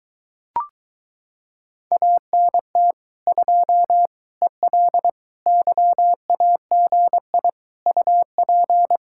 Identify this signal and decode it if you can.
Morse code, heard as a keyed tone, about 22 wpm, 700 Hz. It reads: ANT 2 EL YAGI UP